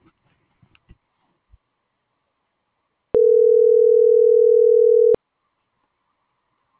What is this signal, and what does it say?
Ringback tone (North America), 440 + 480 Hz.
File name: CENTRAL GENERAL / out alexander.sanchez